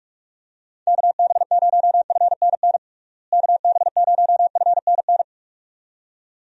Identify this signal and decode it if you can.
Morse code, heard as a keyed tone, about 42 wpm, 700 Hz. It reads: KB0FNN KB0FNN
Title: USA Call Signs - 6 Characters - Repeated 2x - 45wpm